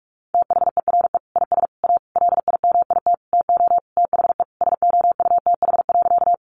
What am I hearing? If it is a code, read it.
Morse code, heard as a keyed tone, about 44 wpm, 700 Hz. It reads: THERE IS A LIMIT TO THE SOUTH.
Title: Sentences from Top 1500 Words 45wpm